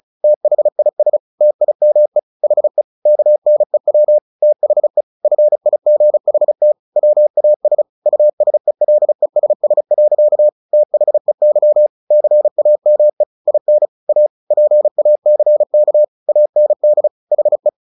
Morse code, heard as a keyed tone, about 34 wpm, 600 Hz. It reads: THIS TIME HE KNEW THE FIGHT WAS USELESS. THEY CAME IN A PACK AND HE
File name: the old man and the sea 2 35wpm